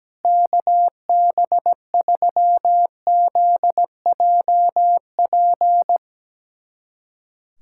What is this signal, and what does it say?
Morse code, heard as a keyed tone, about 17 wpm, 700 Hz. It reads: KB3ZJP
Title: USA Callsign DE Callsign 17wpm